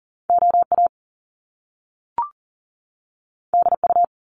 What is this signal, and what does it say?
Morse code, heard as a keyed tone, about 40 wpm, 700 Hz. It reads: OA DU
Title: Top Two Letter Combinations in Words 40wpm